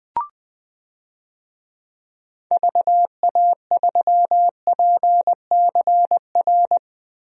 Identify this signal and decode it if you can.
Morse code, heard as a keyed tone, about 20 wpm, 700 Hz. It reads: VA3PCR
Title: DXCC-100 Call Signs - 6 Characters - Repeated 3x - 20wpm